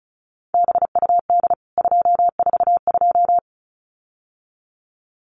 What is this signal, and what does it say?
Morse code, heard as a keyed tone, about 34 wpm, 700 Hz. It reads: BUD 242
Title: CWT Exchange 35wpm